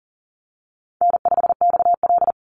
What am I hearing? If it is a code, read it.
Morse code, heard as a keyed tone, about 39 wpm, 700 Hz. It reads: N5XL
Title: NA SOTA Chaser Callsigns 40wpm